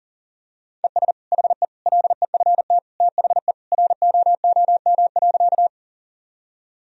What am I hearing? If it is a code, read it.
Morse code, heard as a keyed tone, about 40 wpm, 700 Hz. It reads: ES HE LEFT THE ROOM.
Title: Sentences from Top 300 Words and Abbreviations 40wpm